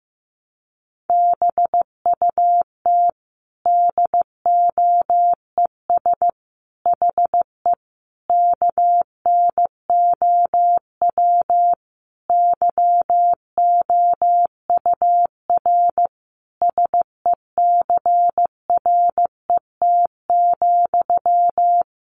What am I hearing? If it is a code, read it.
Morse code, heard as a keyed tone, about 15 wpm, 700 Hz. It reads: BUT DOES HE KNOW YOUR SECRET,
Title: Sentences from Top 1200 Words 15wpm